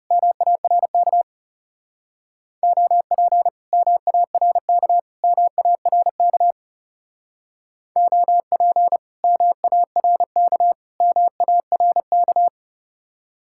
Morse code, heard as a keyed tone, about 39 wpm, 700 Hz. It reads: MARK OP MARK MARK OP MARK MARK